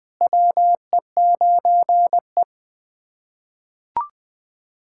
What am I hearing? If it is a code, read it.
Morse code, heard as a keyed tone, about 21 wpm, 700 Hz. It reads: WE9E
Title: SOTA QSO Practice 20wpm